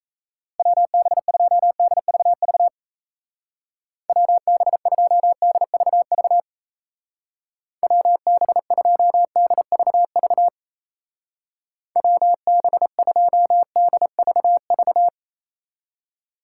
Morse code, heard as a keyed tone, about 36 wpm, 700 Hz. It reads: WB2DVV WB2DVV WB2DVV WB2DVV